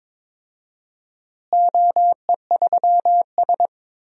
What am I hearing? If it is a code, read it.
Morse code, heard as a keyed tone, about 22 wpm, 700 Hz. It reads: OE3S
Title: DXCC-100 Call Signs - 4 Characters - Repeated 3x - 22wpm